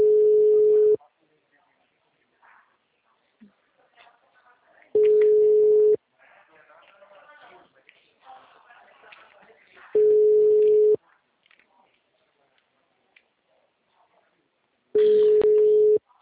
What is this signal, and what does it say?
Ringback tone (Europe, 425 Hz), with speech.